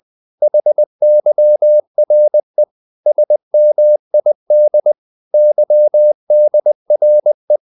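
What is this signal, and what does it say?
Morse code, heard as a keyed tone, about 20 wpm, 600 Hz. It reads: HYRE SMID YDRE
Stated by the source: Morse code practice of short Danish words at 20 wpm, from CW med Gnister.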